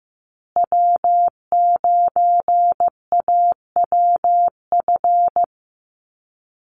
Morse code, heard as a keyed tone, about 15 wpm, 700 Hz. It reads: W9AWF
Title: USA Call Signs Repeated 2x 15wpm